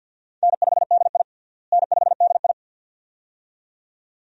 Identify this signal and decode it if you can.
Morse code, heard as a keyed tone, about 48 wpm, 700 Hz. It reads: N5DI N5DI
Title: USA Call Signs - 4 Characters - Repeated 2x - 50wpm